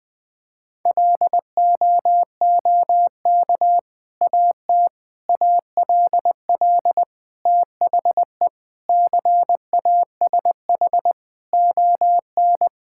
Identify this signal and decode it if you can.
Morse code, heard as a keyed tone, about 20 wpm, 700 Hz. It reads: LOOK AT ALL THE CASH ON
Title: Sentences from Top 1000 Words - Review Part 5 - 20wpm